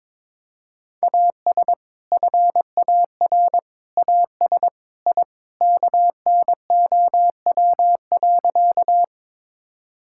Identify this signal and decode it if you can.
Morse code, heard as a keyed tone, about 22 wpm, 700 Hz. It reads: AS FAR AS I KNOW.
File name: practice / 22wpm / Sentences with 3-Word Phrases - 22wpm